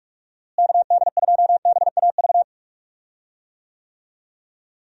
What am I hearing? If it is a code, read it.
Morse code, heard as a keyed tone, about 41 wpm, 700 Hz. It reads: KD2BAV
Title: USA Call Signs - 6 Characters - Repeated 3x - 45wpm